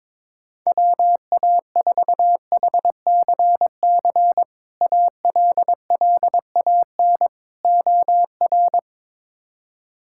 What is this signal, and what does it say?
Morse code, heard as a keyed tone, about 22 wpm, 700 Hz. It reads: WA4HCC ALLAN OR
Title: K1USN SST QSO Practice - 22wpm